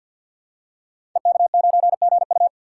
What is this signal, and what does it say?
Morse code, heard as a keyed tone, about 52 wpm, 700 Hz. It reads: EK9GU
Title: Code Groups - 5 Letter:Numbers - 50wpm